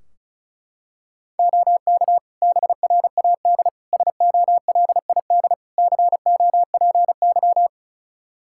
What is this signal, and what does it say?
Morse code, heard as a keyed tone, about 36 wpm, 700 Hz. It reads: OK BRAD SOLID COPY
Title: Practice QSO - Exchange 3 - More Info - 35wpm